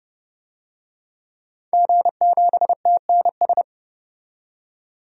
Morse code, heard as a keyed tone, about 30 wpm, 700 Hz. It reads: G7TNS
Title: DXCC-100 Call Signs - 5 Characters - 30wpm